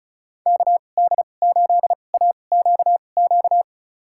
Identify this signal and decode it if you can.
Morse code, heard as a keyed tone, about 36 wpm, spaced as for 25 wpm, 700 Hz. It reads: KD8AQQ
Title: USA Call Signs - 6 Characters - 2x Character Spacing - 35wpm